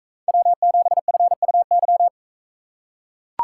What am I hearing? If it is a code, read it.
Morse code, heard as a keyed tone, about 43 wpm, 700 Hz. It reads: W7FUY